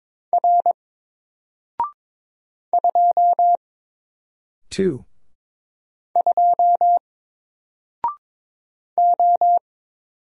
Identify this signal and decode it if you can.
Morse code, heard as a keyed tone, about 23 wpm, 700 Hz. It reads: R 2 2 O
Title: Single Character (Letter:Number:Punctuation:Prosign) 22wpm